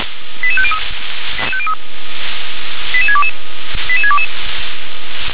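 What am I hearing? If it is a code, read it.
Five-tone selective call (ZVEI), heard as quick runs of five tones, tones of about 80 ms. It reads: 88503 88622 88622
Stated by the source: SDR receiver recording of ZVEI five-tone selective call 88730.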